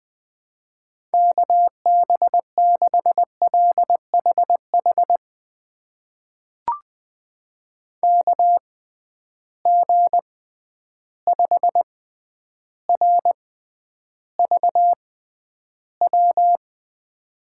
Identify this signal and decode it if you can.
Morse code, heard as a keyed tone, about 20 wpm, spaced as for 10 wpm, 700 Hz. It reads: KB6LHH KG5RVW
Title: USA Call Signs - 6 Characters - 6x Character Spacing - 20wpm